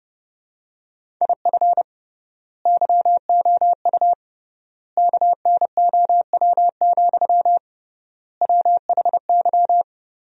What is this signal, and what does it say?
Morse code, heard as a keyed tone, about 30 wpm, 700 Hz. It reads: IF YOU KNOW, WHY
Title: 3x Word-Spacing - Sentences from Top 300 Words - 30wpm